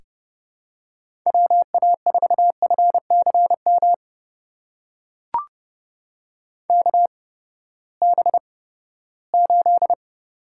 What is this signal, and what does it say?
Morse code, heard as a keyed tone, about 30 wpm, spaced as for 17 wpm, 700 Hz. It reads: WA4FCM KB8
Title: USA Call Signs - 6 Characters - 8x Character Spacing - 30wpm